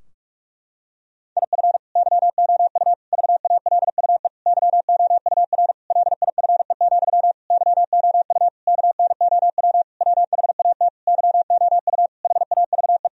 Morse code, heard as a keyed tone, about 46 wpm, 700 Hz. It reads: IF YOU VALUE YOUR LIFE, YOU KNOW WHAT YOU HAVE
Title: Sentences from Top 1000 Words - Review Part 3 - 45wpm